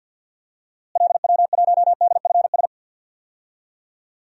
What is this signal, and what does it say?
Morse code, heard as a keyed tone, about 50 wpm, 700 Hz. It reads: LW1DFS
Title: DXCC-100 Call Signs - 6 Characters - Repeated 2x - 50wpm